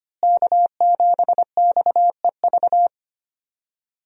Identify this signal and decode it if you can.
Morse code, heard as a keyed tone, about 25 wpm, 700 Hz. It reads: K7XEV
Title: USA Call Signs - 5 Characters - Repeated 2x - 25wpm